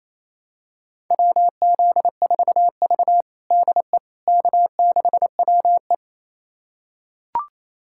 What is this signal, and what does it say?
Morse code, heard as a keyed tone, about 28 wpm, 700 Hz. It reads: WZ4V DE K6WE
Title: USA Callsign DE Callsign - 4 Characters - 28wpm